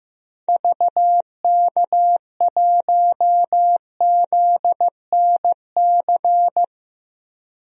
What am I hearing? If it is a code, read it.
Morse code, heard as a keyed tone, about 15 wpm, 700 Hz. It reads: VK1ZNC
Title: DXCC-100 Call Signs - 6 Characters - 15wpm